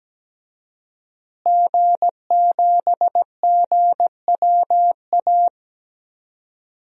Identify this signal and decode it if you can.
Morse code, heard as a keyed tone, about 17 wpm, 700 Hz. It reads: G7GWA
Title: DXCC-100 Call Signs - 5 Characters - Repeated 3x - 17wpm